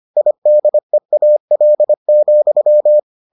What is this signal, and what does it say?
Morse code, heard as a keyed tone, about 25 wpm, 600 Hz. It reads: IDEAL,